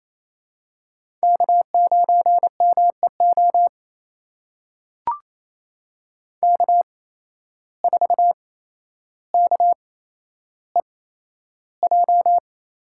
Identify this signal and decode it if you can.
Morse code, heard as a keyed tone, about 28 wpm, spaced as for 12 wpm, 700 Hz. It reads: K9MEO K4KEJ
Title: USA Call Signs - 5 Characters - 8x Character Spacing - 28wpm